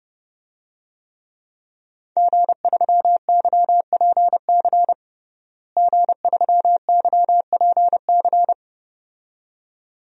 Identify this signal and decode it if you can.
Morse code, heard as a keyed tone, about 30 wpm, 700 Hz. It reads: G3YPC G3YPC
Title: DXCC-100 Call Signs - 5 Characters - Repeated 2x - 30wpm